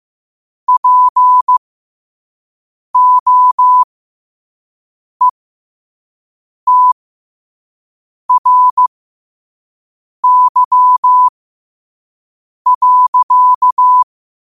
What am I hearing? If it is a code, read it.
Morse code, heard as a keyed tone, about 15 wpm, spaced as for 5 wpm, 1000 Hz. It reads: POETRY.
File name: qotdfiles / 5wpm